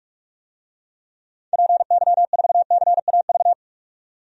Morse code, heard as a keyed tone, about 46 wpm, 700 Hz. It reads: PY4CAV
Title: DXCC-100 Call Signs - 6 Characters - 45wpm